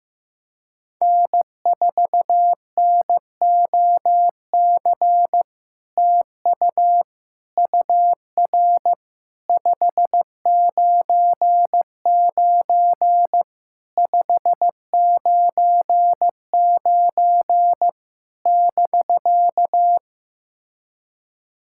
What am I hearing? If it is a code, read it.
Morse code, heard as a keyed tone, about 15 wpm, 700 Hz. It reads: N4NOC TU UR 599 599 <BK>
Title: SOTA QSO Practice 15wpm